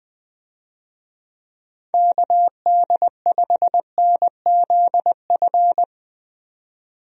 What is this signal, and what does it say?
Morse code, heard as a keyed tone, about 20 wpm, 700 Hz. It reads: KD5NZF
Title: USA Call Signs - 6 Characters - 20wpm